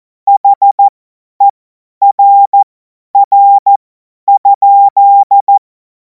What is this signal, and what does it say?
Morse code, heard as a keyed tone, about 14 wpm, spaced as for 10 wpm, 800 Hz. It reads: HERR?
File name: Morsekurs / Witze / Witz 17. Februar 2026 50 F70